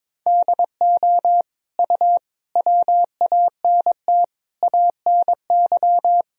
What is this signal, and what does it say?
Morse code, heard as a keyed tone, about 22 wpm, 700 Hz. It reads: DO U WANT ANY
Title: Sentences from Top 400 Words and Abbreviations 22wpm